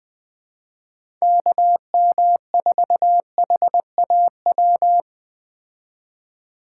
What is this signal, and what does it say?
Morse code, heard as a keyed tone, about 20 wpm, 700 Hz. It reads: KM4HAW
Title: Speed Racing - USA Call Signs 17wpm